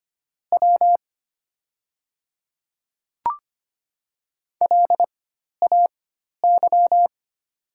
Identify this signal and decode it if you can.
Morse code, heard as a keyed tone, about 25 wpm, spaced as for 12 wpm, 690 Hz. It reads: W LAY
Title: 4x Character Spacing - Top 500 Words - 25wpm